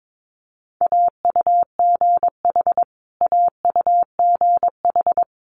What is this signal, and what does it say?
Morse code, heard as a keyed tone, about 22 wpm, 700 Hz. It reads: AUGH AUGH
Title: Multi-letter Phonemes Repeated 3x 22wpm